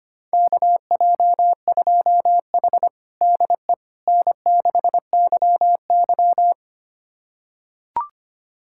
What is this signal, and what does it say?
Morse code, heard as a keyed tone, about 25 wpm, 700 Hz. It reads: KJ2H DE N6YY
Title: USA Callsign DE Callsign - 4 Characters - 25wpm